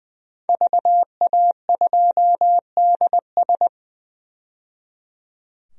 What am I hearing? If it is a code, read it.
Morse code, heard as a keyed tone, about 20 wpm, 700 Hz. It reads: VA2DS
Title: DXCC-100 Call Signs - 5 Characters - 20wpm